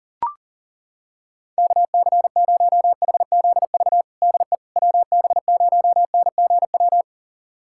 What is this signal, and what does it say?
Morse code, heard as a keyed tone, about 40 wpm, 690 Hz. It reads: KC0HZV DE WB0NGW
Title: USA Callsign DE Callsign - 6 Characters - 40wpm